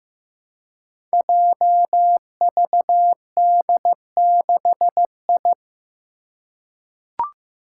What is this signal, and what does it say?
Morse code, heard as a keyed tone, about 16 wpm, 700 Hz. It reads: JVD6I
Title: Five Challenging Letter-Number Combinations 15wpm